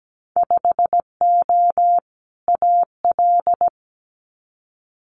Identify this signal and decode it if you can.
Morse code, heard as a keyed tone, about 17 wpm, 700 Hz. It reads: 5O AL
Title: Winter Field Day Exchanges 17wpm